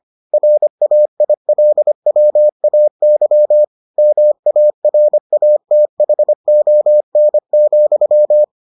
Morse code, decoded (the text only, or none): RAILWAY MARATHON,